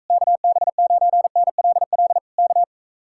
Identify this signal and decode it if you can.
Morse code, heard as a keyed tone, about 45 wpm, 700 Hz. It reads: KB9NLL K